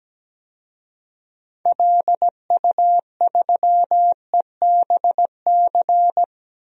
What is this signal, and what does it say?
Morse code, heard as a keyed tone, about 17 wpm, 700 Hz. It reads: LU3EBC
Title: DXCC-100 Call Signs - 6 Characters - Repeated 3x - 17wpm